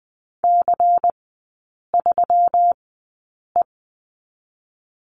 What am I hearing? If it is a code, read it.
Morse code, heard as a keyed tone, about 20 wpm, spaced as for 8 wpm, 700 Hz. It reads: C3E
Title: 138 Lesson 23e - Letter f - Three Characters 20wpm